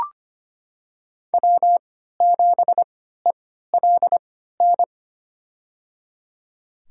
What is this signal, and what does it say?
Morse code, heard as a keyed tone, about 25 wpm, spaced as for 14 wpm, 700 Hz. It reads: W7ELN
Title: USA Call Signs - 5 Characters - 3x Character Spacing - 25wpm